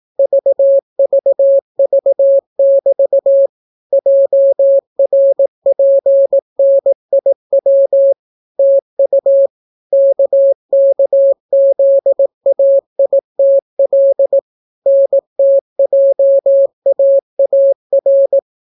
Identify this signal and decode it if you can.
Morse code, heard as a keyed tone, about 18 wpm, 550 Hz. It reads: VVV<BT> JRPNIW TU KKZAITL NTJAAR